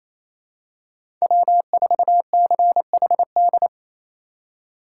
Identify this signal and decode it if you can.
Morse code, heard as a keyed tone, about 28 wpm, 700 Hz. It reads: W4CHD
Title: USA Call Signs - 5 Characters - 4x Character Spacing - 28wpm